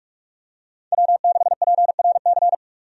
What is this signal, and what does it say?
Morse code, heard as a keyed tone, about 44 wpm, 700 Hz. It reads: W6PRC
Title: USA Call Signs - 5 Characters - 45wpm